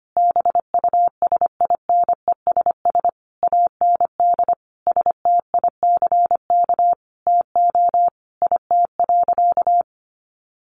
Morse code, heard as a keyed tone, about 25 wpm, 700 Hz. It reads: BUSINESS AND STICK TO IT.
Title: Sentences from Top 1200 Words 25wpm